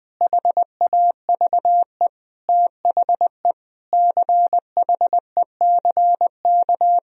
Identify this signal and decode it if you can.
Morse code, heard as a keyed tone, about 20 wpm, 700 Hz. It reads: HAVE THE CHECK
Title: Sentences from Top 1000 Words - Review Part 3 - 20wpm